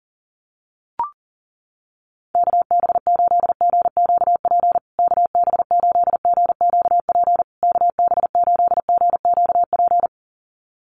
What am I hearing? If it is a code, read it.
Morse code, heard as a keyed tone, about 40 wpm, 700 Hz. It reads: KB8GQP KB8GQP KB8GQP
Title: USA Call Signs Repeated 3x 40wpm